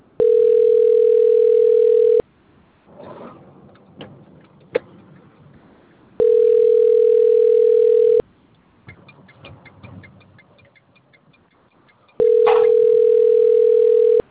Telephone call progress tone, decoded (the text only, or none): ringback tone